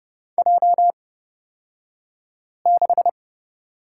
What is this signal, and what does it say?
Morse code, heard as a keyed tone, about 30 wpm, spaced as for 5 wpm, 700 Hz. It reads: J6
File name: Lesson 36d - BT - Two Characters 30wpm